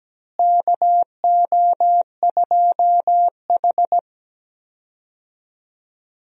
Morse code, heard as a keyed tone, about 17 wpm, 700 Hz. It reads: KO2H
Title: USA Call Signs - 4 Characters - Repeated 3x - 17wpm